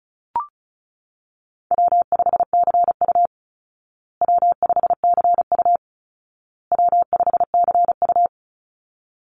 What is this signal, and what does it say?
Morse code, heard as a keyed tone, about 34 wpm, 700 Hz. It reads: W5CU W5CU W5CU
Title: Easy Call Signs Repeated 3x - 35wpm